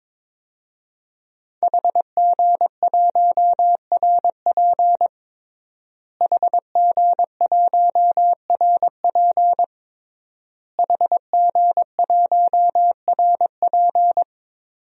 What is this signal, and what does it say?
Morse code, heard as a keyed tone, about 22 wpm, 700 Hz. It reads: HG1RP HG1RP HG1RP
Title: DXCC-100 Call Signs - 5 Characters - Repeated 3x - 22wpm